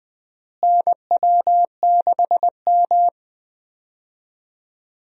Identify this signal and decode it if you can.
Morse code, heard as a keyed tone, about 21 wpm, 700 Hz. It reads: NW6M
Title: USA Call Signs - 4 Characters - Repeated 3x - 20wpm